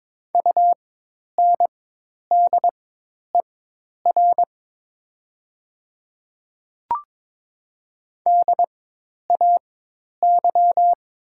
Morse code, heard as a keyed tone, about 22 wpm, spaced as for 10 wpm, 700 Hz. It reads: UNDER DAY
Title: 4x Character Spacing - Top 500 Words - 22wpm